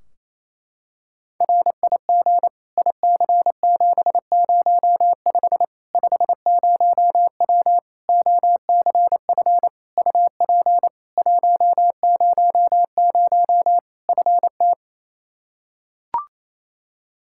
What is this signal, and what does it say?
Morse code, heard as a keyed tone, about 28 wpm, 700 Hz. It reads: RIG IC705 50W OCF UP 100 FT
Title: Practice QSO - Exchange 2 - Info - 28wpm